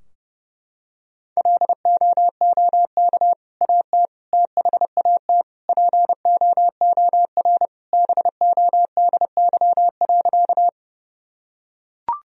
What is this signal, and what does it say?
Morse code, heard as a keyed tone, about 30 wpm, 700 Hz. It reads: LOOK AT THAT POOR BODY.
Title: Sentences from Top 1000 Words - Review Part 2 - 30wpm